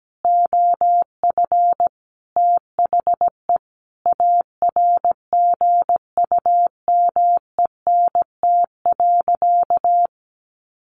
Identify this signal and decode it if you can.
Morse code, heard as a keyed tone, about 17 wpm, 700 Hz. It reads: OF THE ARGUMENT.
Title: Sentences from Top 1400 Words 17wpm